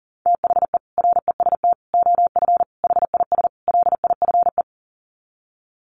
Morse code, heard as a keyed tone, about 40 wpm, 700 Hz. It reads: THE REST OF HIS LIFE
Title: Sets of 5 Words 40wpm